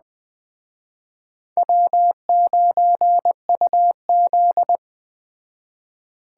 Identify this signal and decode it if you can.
Morse code, heard as a keyed tone, about 21 wpm, 700 Hz. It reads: W9UZ
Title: Speed Racing - USA Call Signs 17wpm